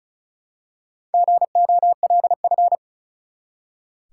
Morse code, heard as a keyed tone, about 35 wpm, 700 Hz. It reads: GOLF
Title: Four Letter Words - 35wpm